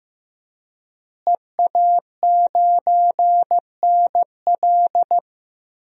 Morse code, heard as a keyed tone, about 15 wpm, 700 Hz. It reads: EA9NL
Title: DXCC-100 Call Signs - 5 Characters - Repeated 3x - 15wpm